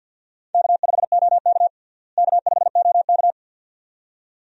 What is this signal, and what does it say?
Morse code, heard as a keyed tone, about 50 wpm, 700 Hz. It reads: K5OK K5OK
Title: USA Call Signs - 4 Characters - Repeated 2x - 50wpm